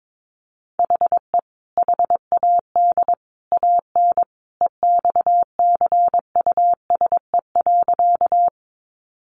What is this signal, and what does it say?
Morse code, heard as a keyed tone, about 22 wpm, 700 Hz. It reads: HE HAD AN EXCUSE.
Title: Sentences from Top 800 Words 22wpm